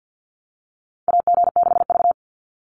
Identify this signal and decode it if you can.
Morse code, heard as a keyed tone, about 50 wpm, 700 Hz. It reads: AG6V
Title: NA SOTA Chaser Callsigns 50wpm